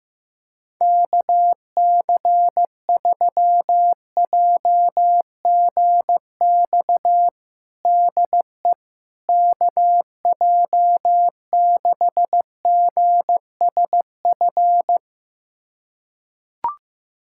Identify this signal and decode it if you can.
Morse code, heard as a keyed tone, about 15 wpm, 700 Hz. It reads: KC3JGX DE KJ6GSF
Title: USA Callsign DE Callsign - 6 Characters - 15wpm